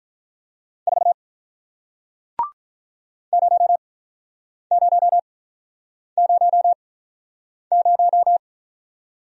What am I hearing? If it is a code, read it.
Morse code, heard as a keyed tone, about 47 wpm, spaced as for 9 wpm, 700 Hz. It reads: 4 0000